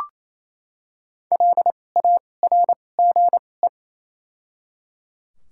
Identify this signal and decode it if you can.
Morse code, heard as a keyed tone, about 28 wpm, 700 Hz. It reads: L A R G E
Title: 2x Character Spacing - Top 200 Words - 28wpm